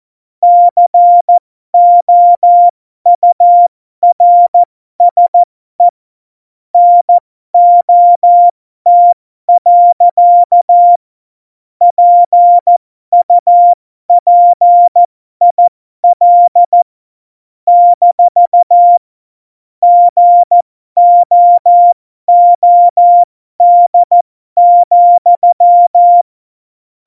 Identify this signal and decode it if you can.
Morse code, heard as a keyed tone, about 14 wpm, spaced as for 12 wpm, 700 Hz. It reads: COURSE NOT. PUPIL - GOOD,
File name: Morsekurs / Witze